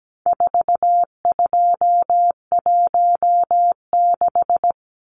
Morse code, heard as a keyed tone, about 17 wpm, 700 Hz. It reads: 4216
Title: Numbers - 4-Digit 17wpm